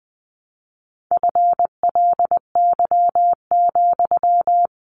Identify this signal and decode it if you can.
Morse code, heard as a keyed tone, about 20 wpm, 700 Hz. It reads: FLY,
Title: Sentences from Top 500 Words 20wpm